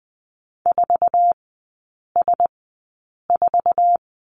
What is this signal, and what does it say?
Morse code, heard as a keyed tone, about 20 wpm, spaced as for 8 wpm, 700 Hz. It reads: 4S4
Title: 060 Lesson 11e - Letter h - Three Characters 20wpm